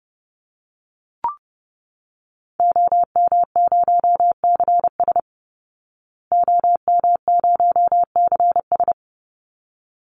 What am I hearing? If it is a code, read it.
Morse code, heard as a keyed tone, about 28 wpm, 700 Hz. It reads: OM0CS OM0CS
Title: Call Signs Repeated 3x - 30wpm